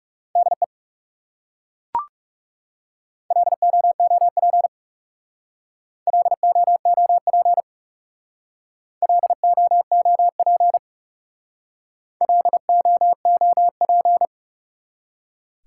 Morse code, heard as a keyed tone, about 38 wpm, 700 Hz. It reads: DE LOOP LOOP LOOP LOOP